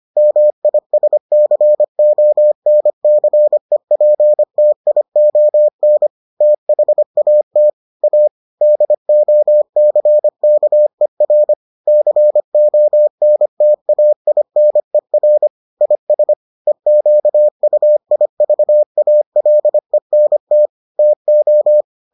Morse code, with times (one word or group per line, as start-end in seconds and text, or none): 0.2-6.1 MISCONCEPTION
6.4-7.7 THAT
8.0-8.3 A
8.6-11.5 DOCKER
11.9-15.5 CONTAINER
15.8-16.3 IS
16.7-20.7 EQUIVALENT
21.0-21.8 TO